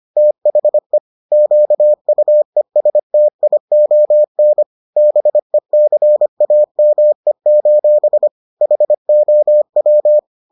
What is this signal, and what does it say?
Morse code, heard as a keyed tone, about 25 wpm, 600 Hz. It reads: THE QUESTION BECAME: HOW